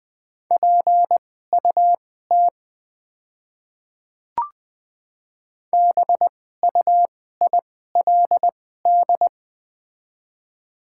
Morse code, heard as a keyed tone, about 20 wpm, spaced as for 15 wpm, 700 Hz. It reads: PUT BUILD
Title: 2x Character Spacing - Top 300 Words - 20wpm